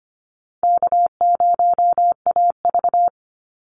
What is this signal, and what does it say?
Morse code, heard as a keyed tone, about 25 wpm, 700 Hz. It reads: K0AV
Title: Easy Call Signs Repeated 3x - 25wpm